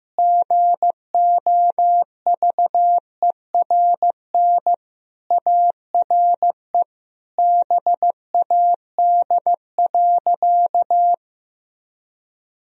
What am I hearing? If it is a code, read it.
Morse code, heard as a keyed tone, about 16 wpm, 700 Hz. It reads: GOVERN ARE BAD.
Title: Sentences from Top 1000 Words - Review Part 2 - 15wpm